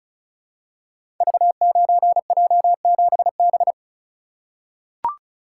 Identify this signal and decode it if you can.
Morse code, heard as a keyed tone, about 34 wpm, 700 Hz. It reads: V9J7B
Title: Code Groups - 5 Letter:Numbers - 35wpm